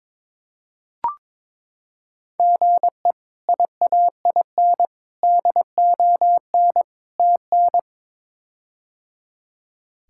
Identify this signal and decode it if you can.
Morse code, heard as a keyed tone, about 22 wpm, 700 Hz. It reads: GE IAIN DON TN
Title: K1USN SST QSO Practice - 22wpm